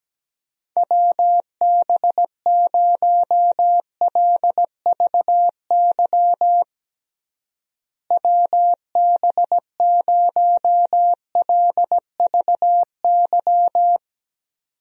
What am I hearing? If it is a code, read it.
Morse code, heard as a keyed tone, about 17 wpm, 700 Hz. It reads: WB0LVY WB0LVY
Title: USA Call Signs - 6 Characters - Repeated 2x - 17wpm